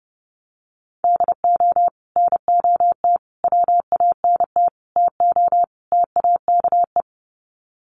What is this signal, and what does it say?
Morse code, heard as a keyed tone, about 30 wpm, 700 Hz. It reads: DO NOT WANT TO TAKE
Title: Sets of 5 Words Encore2 30wpm